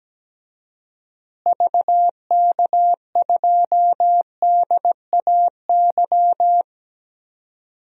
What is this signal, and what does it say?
Morse code, heard as a keyed tone, about 17 wpm, 700 Hz. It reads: VK2DAY
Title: DXCC-100 Call Signs - 6 Characters - Repeated 3x - 17wpm